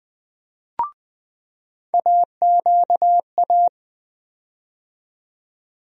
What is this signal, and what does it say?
Morse code, heard as a keyed tone, about 20 wpm, 700 Hz. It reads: AQA
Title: USA Call Sign Postfixes - 3 Letters 20wpm